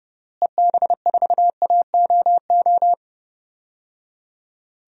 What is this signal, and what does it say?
Morse code, heard as a keyed tone, about 32 wpm, 700 Hz. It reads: EB4AOO
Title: DXCC-100 Call Signs - 6 Characters - Repeated 3x - 30wpm